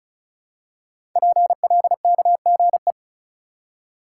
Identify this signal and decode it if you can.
Morse code, heard as a keyed tone, about 36 wpm, 700 Hz. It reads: PLKGE
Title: Code Groups - 5 Letters - 35wpm